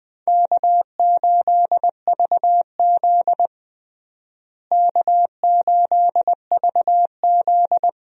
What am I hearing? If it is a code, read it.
Morse code, heard as a keyed tone, about 20 wpm, 700 Hz. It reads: K8VZ K8VZ
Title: USA Call Signs - 4 Characters - Repeated 3x - 20wpm